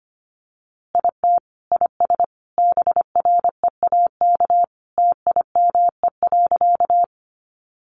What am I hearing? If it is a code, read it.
Morse code, heard as a keyed tone, about 25 wpm, 700 Hz. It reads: IT IS BREAK TIME.
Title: Easy Phrases 25wpm